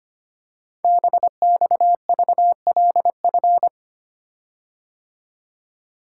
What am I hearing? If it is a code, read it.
Morse code, heard as a keyed tone, about 25 wpm, 700 Hz. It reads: BXVLF
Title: Code Groups - 5 Letters - 25wpm